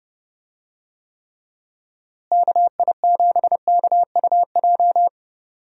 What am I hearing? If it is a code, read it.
Morse code, heard as a keyed tone, about 30 wpm, 700 Hz. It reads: KI7KUJ
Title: USA Call Signs - 6 Characters - Repeated 3x - 30wpm